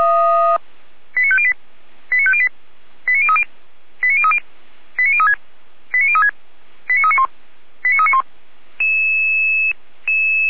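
Five-tone selective call (ZVEI), heard as quick runs of five tones, tones of about 70 ms. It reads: 89508 89508 89030 89030 89036 89036 89391 89391
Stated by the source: ZVEI five-tone selective call received by SDR.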